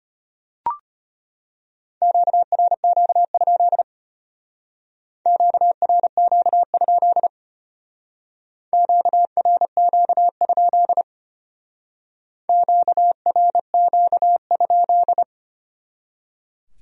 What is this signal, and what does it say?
Morse code, heard as a keyed tone, about 33 wpm, 700 Hz. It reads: QRQ? QRQ? QRQ? QRQ?